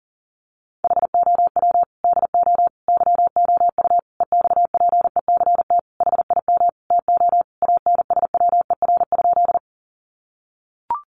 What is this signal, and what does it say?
Morse code, heard as a keyed tone, about 40 wpm, 700 Hz. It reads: HOW DO YOU EXPECT HIM TO ANSWER?
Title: Sentences from Top 1100 Words 40wpm